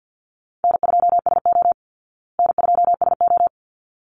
Morse code, heard as a keyed tone, about 50 wpm, 700 Hz. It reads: N2SO N2SO
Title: Easy Call Signs Repeated 2x - 50wpm